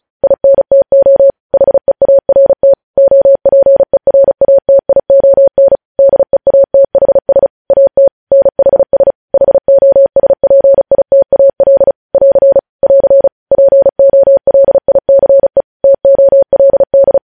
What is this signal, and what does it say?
Morse code, heard as a keyed tone, about 36 wpm, 550 Hz. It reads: INTO HEART OPERATION DEATHS AT NHS HOSPITAL <AR> <AR> POLICE TOLD
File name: podcast / News headlines-1x-35x35